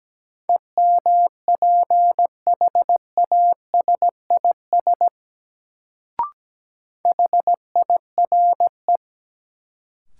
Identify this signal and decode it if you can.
Morse code, heard as a keyed tone, about 17 wpm, 700 Hz. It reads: EMPHASIS HIRE
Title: Sentences from Top 1400 Words - New Word Drill Alt - 17wpm